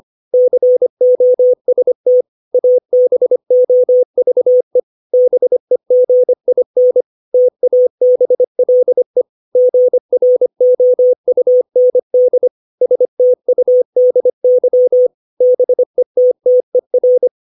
Morse code, heard as a keyed tone, about 25 wpm, 500 Hz. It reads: COST ABOVE BEGIN TABLE GROUND STUDY BETTER
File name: 632 words / 632 Words 25WPM